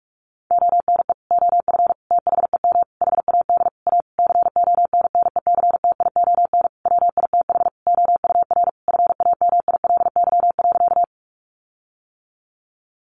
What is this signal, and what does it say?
Morse code, heard as a keyed tone, about 44 wpm, 700 Hz. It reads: ONE OF THEM HAD A CONNECTION WITH OUR FAMILY.
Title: Sentences from Top 1300 Words 45wpm